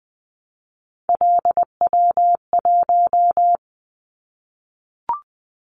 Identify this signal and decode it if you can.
Morse code, heard as a keyed tone, about 20 wpm, 700 Hz. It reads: LW1
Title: Callsign Prefixes for Top 100 DXCC - 20wpm